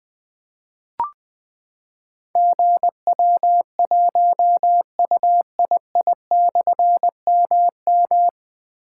Morse code, heard as a keyed tone, about 20 wpm, 700 Hz. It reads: GW1UII/MM
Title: Difficult Call Signs 20wpm